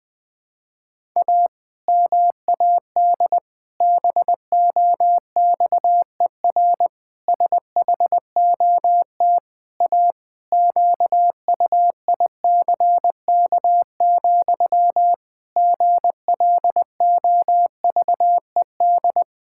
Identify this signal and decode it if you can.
Morse code, heard as a keyed tone, about 20 wpm, 700 Hz. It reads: A MAD BOXER SHOT A QUICK, GLOVED
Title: Pangram Challenge v2 - 20wpm